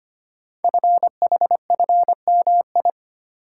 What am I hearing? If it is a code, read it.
Morse code, heard as a keyed tone, about 25 wpm, 700 Hz. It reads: FHFMI
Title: Code Groups - 5 Letter:Numbers - 25wpm